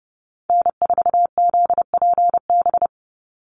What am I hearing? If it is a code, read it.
Morse code, heard as a keyed tone, about 30 wpm, 700 Hz. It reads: N4ZPB
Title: USA Call Signs 30wpm